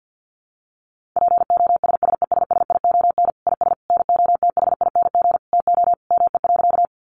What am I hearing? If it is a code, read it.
Morse code, heard as a keyed tone, about 48 wpm, 700 Hz. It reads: POSSESSION IS NOTHING TO ME.
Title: Sentences from Top 1400 Words 50wpm